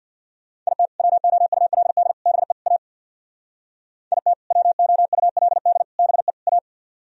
Morse code, heard as a keyed tone, about 64 wpm, 700 Hz. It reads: IT WOULD BE A IT WOULD BE A